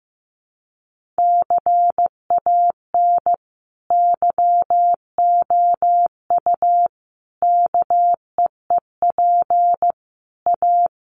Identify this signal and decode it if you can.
Morse code, heard as a keyed tone, about 15 wpm, 700 Hz. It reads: CAN YOU KEEP A
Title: Sentences from Top 1200 Words 15wpm